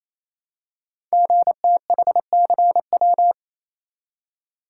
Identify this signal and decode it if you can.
Morse code, heard as a keyed tone, about 27 wpm, 700 Hz. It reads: GTHCW
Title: Code Groups - 5 Letters - 28wpm